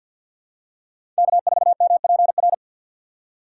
Morse code, heard as a keyed tone, about 51 wpm, 700 Hz. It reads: K4MPR
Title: USA Call Signs - 5 Characters - 3x Character Spacing - 50wpm